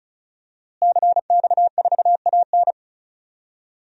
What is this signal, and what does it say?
Morse code, heard as a keyed tone, about 36 wpm, 700 Hz. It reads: CX4AN
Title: DXCC-100 Call Signs - 5 Characters - Repeated 2x - 35wpm